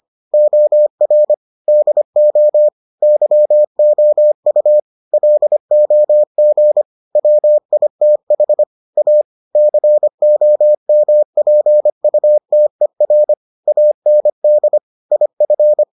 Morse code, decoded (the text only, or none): OR DO YOU LOG WITH A COMPUTER AND IF